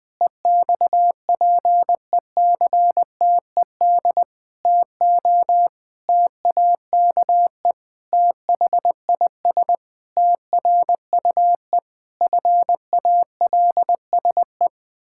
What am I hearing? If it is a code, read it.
Morse code, heard as a keyed tone, about 20 wpm, 700 Hz. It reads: EXPECTED TO TAKE THIS TRUE FALSE
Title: Pangram Challenge v2 - 20wpm